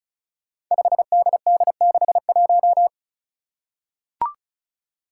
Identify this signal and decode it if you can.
Morse code, heard as a keyed tone, about 34 wpm, 700 Hz. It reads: 5DD61
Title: Five Challenging Letter-Number Combinations 35wpm